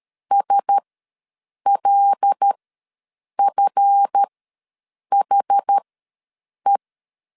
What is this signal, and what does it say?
Morse code, heard as a keyed tone, about 13 wpm, spaced as for 7 wpm, 800 Hz. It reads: SLFHE
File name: LEZIONI / 7 GF9Z / 7 GF ws 12wpm 3140